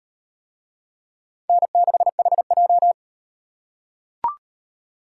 Morse code, heard as a keyed tone, about 38 wpm, 700 Hz. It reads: N6HJ